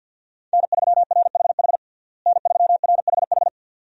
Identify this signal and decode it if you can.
Morse code, heard as a keyed tone, about 49 wpm, 700 Hz. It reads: N3RHH N3RHH
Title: USA Call Signs - 5 Characters - Repeated 2x - 50wpm